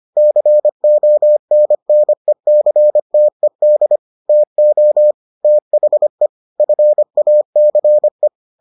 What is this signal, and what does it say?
Morse code, heard as a keyed tone, about 25 wpm, 600 Hz. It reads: CONNECTED TO THE FACE